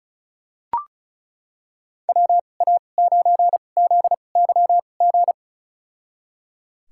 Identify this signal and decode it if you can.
Morse code, heard as a keyed tone, about 36 wpm, spaced as for 25 wpm, 700 Hz. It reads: WA9ZYG
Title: USA Call Signs - 2x Character Spacing - 35wpm